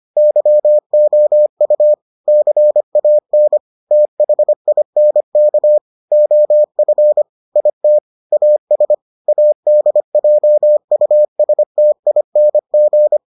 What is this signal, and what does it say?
Morse code, heard as a keyed tone, about 25 wpm, 600 Hz. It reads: YOU CAN THINK OF IT AS ADJUSTING